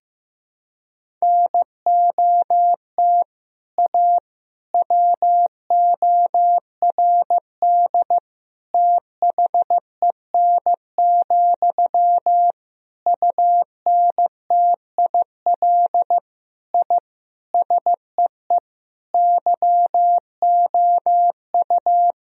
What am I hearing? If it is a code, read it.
Morse code, heard as a keyed tone, about 15 wpm, 700 Hz. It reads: NOT A WORD THEN, UNTIL I SEE YOU
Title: Sentences from Top 1000 Words - Review Part 1 - 15wpm